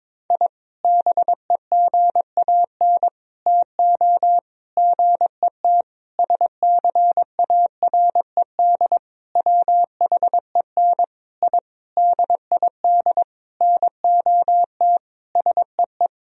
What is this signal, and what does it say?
Morse code, heard as a keyed tone, about 22 wpm, 700 Hz. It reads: I BEGAN TO GET SCARED WHEN I DID NOT SEE
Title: Sentences from Top 1000 Words - Review Part 4 - 22wpm